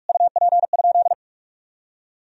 Morse code, heard as a keyed tone, about 46 wpm, 700 Hz. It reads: UP?
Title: Sentences from Top 1000 Words - Review Part 1 - 45wpm